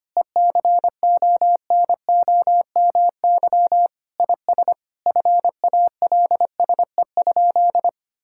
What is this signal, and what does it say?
Morse code, heard as a keyed tone, about 25 wpm, 700 Hz. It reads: ECONOMY IS FALSE?
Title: Sentences from Top 1600 Words 25wpm